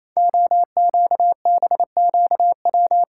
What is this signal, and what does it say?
Morse code, heard as a keyed tone, about 28 wpm, 690 Hz. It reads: OQBQW
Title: Code Groups - 5 Letters - 28wpm